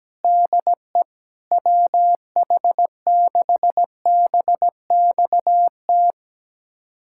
Morse code, heard as a keyed tone, about 17 wpm, 700 Hz. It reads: DE WH6BXT
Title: USA Callsign DE Callsign 17wpm